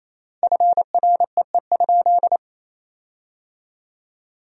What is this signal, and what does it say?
Morse code, heard as a keyed tone, about 28 wpm, 700 Hz. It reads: FREE?
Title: 4x Word-Spacing - Sentences from Top 500 Words - 28wpm